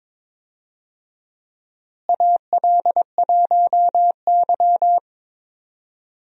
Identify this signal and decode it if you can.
Morse code, heard as a keyed tone, about 22 wpm, 700 Hz. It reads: AL1Y
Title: USA Call Signs - 4 Characters - Repeated 3x - 22wpm